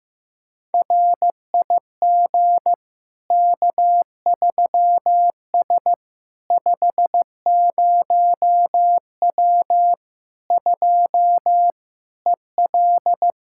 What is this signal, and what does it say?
Morse code, heard as a keyed tone, about 15 wpm, 700 Hz. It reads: RIG K3S 50W 2 EL
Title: Practice QSO - Exchange 2 - Info - 15wpm